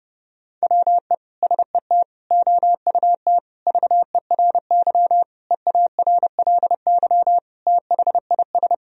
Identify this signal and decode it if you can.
Morse code, heard as a keyed tone, about 32 wpm, 700 Hz. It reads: WE SET OUT VERY EARLY THIS
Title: Sentences from Top 1000 Words - Review Part 2 - 30wpm